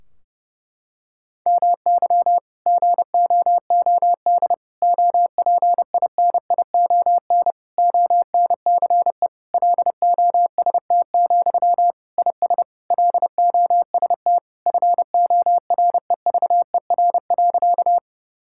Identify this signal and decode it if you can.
Morse code, heard as a keyed tone, about 30 wpm, 700 Hz. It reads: MY GOOD OPINION ONCE LOST, IS LOST FOREVER.
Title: Sentences from Top 1600 Words 30wpm